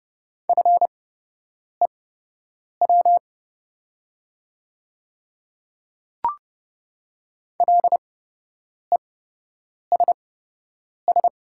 Morse code, heard as a keyed tone, about 31 wpm, spaced as for 8 wpm, 700 Hz. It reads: FEW LESS
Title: 8x Character Spacing - Top 500 Words - 30wpm